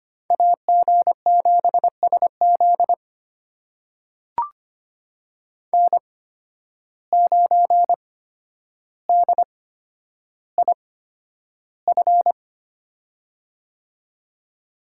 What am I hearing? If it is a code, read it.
Morse code, heard as a keyed tone, about 25 wpm, spaced as for 11 wpm, 700 Hz. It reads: AG7SZ N9DIF
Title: USA Call Signs - 5 Characters - 8x Character Spacing - 25wpm